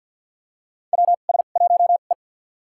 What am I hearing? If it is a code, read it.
Morse code, heard as a keyed tone, about 52 wpm, spaced as for 37 wpm, 700 Hz. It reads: WS1E
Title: USA Call Signs - 4 Characters - 2x Character Spacing - 50wpm